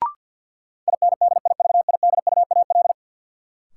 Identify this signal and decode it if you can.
Morse code, heard as a keyed tone, about 51 wpm, 700 Hz. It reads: INDIVIDUAL
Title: Sentences from Top 1000 Words - Review Part 4 - New Word Drill - 50wpm